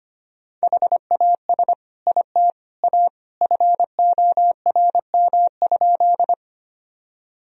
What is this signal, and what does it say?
Morse code, heard as a keyed tone, about 25 wpm, 700 Hz. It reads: HAS IT A FORM?
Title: Sentences from Top 1000 Words - Review Part 1 - 25wpm